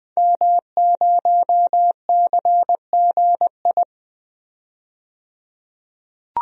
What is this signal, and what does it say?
Morse code, heard as a keyed tone, about 20 wpm, 700 Hz. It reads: M0CGI
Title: DXCC-100 Call Signs - 5 Characters - 20wpm